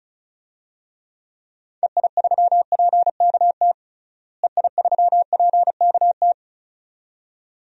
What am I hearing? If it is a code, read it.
Morse code, heard as a keyed tone, about 36 wpm, 700 Hz. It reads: EI3PKT EI3PKT
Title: DXCC-100 Call Signs - 6 Characters - Repeated 2x - 35wpm